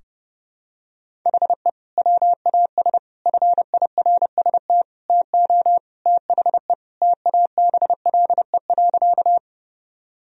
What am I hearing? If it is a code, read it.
Morse code, heard as a keyed tone, about 30 wpm, 700 Hz. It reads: HE WAS FIRST TO THE TABLE.
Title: Sentences from Top 1000 Words - Review Part 2 - 30wpm